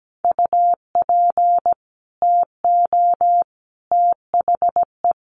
Morse code, heard as a keyed tone, about 17 wpm, 700 Hz. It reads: UP TO THE
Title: Sets of 3 Words 17wpm